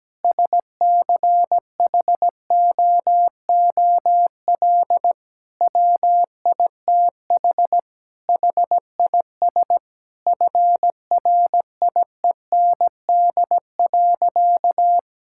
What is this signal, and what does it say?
Morse code, heard as a keyed tone, about 17 wpm, 700 Hz. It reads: SCHOOL WITH HIS FRIEND.
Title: Sentences from Top 1600 Words 17wpm